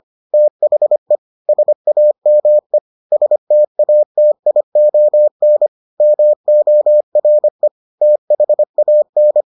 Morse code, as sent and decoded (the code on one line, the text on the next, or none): - .... / ... .- -- . / ... - .- - .. --- -. / -- --- .-. / - .... .- -.
THE SAME STATION MORE THAN